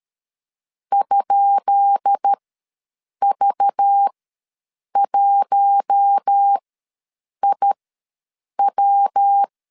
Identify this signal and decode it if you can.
Morse code, heard as a keyed tone, about 13 wpm, spaced as for 7 wpm, 800 Hz. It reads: ?V1IW
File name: LEZIONI / 10 SIGNS / 10C DOT ws 12wpm 3548